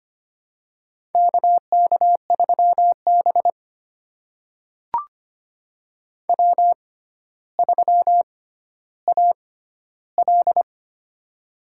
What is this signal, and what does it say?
Morse code, heard as a keyed tone, about 25 wpm, spaced as for 13 wpm, 700 Hz. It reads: KK3B W3AL
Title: USA Call Signs - 4 Characters - 6x Character Spacing - 25wpm